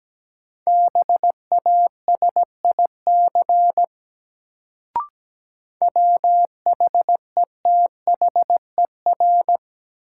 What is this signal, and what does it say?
Morse code, heard as a keyed tone, about 17 wpm, 700 Hz. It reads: BASIC WHETHER
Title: Sentences from Top 1000 Words - Review Part 3 - New Word Drill - 17wpm